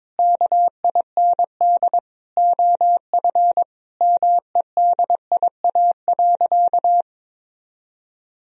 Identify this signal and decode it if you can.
Morse code, heard as a keyed tone, about 22 wpm, 700 Hz. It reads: KIND OF MEDIA.
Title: Sentences from Top 1000 Words - Review Part 4 - 22wpm